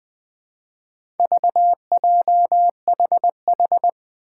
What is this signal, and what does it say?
Morse code, heard as a keyed tone, about 20 wpm, 700 Hz. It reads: VJHH
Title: Four Challenging Letter-Number Combinations 20wpm